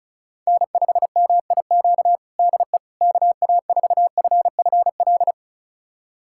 Morse code, heard as a keyed tone, about 34 wpm, 700 Hz. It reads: N5MIQ DE KA4FFL
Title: USA Callsign DE Callsign 35wpm